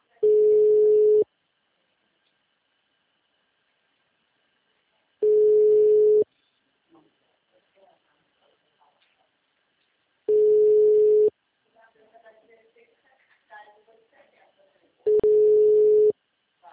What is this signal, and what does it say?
Ringback tone (Europe, 425 Hz), with speech.
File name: out- -2031- - - .52